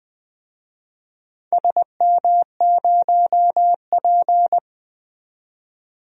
Morse code, heard as a keyed tone, about 20 wpm, 700 Hz. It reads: SM0P
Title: DXCC-100 Call Signs - 4 Characters - 20wpm